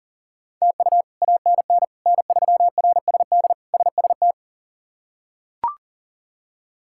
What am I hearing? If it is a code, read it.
Morse code, heard as a keyed tone, about 40 wpm, 700 Hz. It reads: TU ANN N3RSD SST
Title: K1USN SST QSO Practice - 40wpm